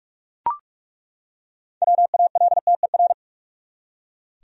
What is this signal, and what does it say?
Morse code, heard as a keyed tone, about 42 wpm, 700 Hz. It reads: WALTER
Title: USA Common Names - 45wpm